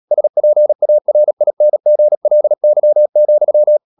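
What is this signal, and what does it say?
Morse code, heard as a keyed tone, about 37 wpm, 600 Hz. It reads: SPARINGLY,
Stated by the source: Morse code practice at 35–39 wpm sending English text of Ken Follett's novel Never, from CW med Gnister.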